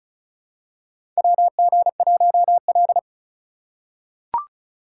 Morse code, heard as a keyed tone, about 36 wpm, 700 Hz. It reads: WG1L
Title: USA Call Signs - 4 Characters - 2x Character Spacing - 35wpm